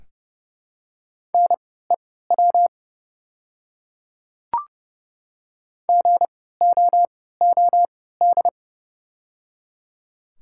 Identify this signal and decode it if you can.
Morse code, heard as a keyed tone, about 31 wpm, spaced as for 17 wpm, 690 Hz. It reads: NEW GOOD
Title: 3x Character Spacing - Top 300 Words - 30wpm